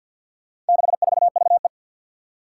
Morse code, heard as a keyed tone, about 46 wpm, 700 Hz. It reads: 64VE
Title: Four Challenging Letter-Number Combinations 50wpm